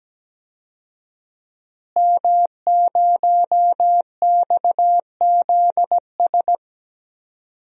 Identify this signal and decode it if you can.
Morse code, heard as a keyed tone, about 17 wpm, 700 Hz. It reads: M0XZS
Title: DXCC-100 Call Signs - 5 Characters - Repeated 2x - 17wpm